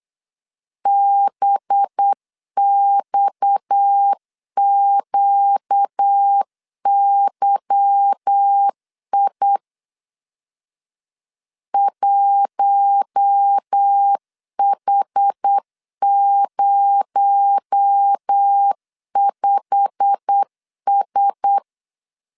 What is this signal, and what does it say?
Morse code, heard as a keyed tone, about 8 wpm, 800 Hz. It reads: BXQYI 1H05S
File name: LEZIONI / 8 YJ45 / 8 YJ45 8wpm 3214